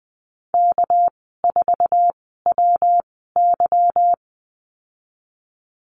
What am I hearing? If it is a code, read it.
Morse code, heard as a keyed tone, about 20 wpm, spaced as for 14 wpm, 700 Hz. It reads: K4WY
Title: 200 Lesson 32g - Letter k - Call Signs 20wpm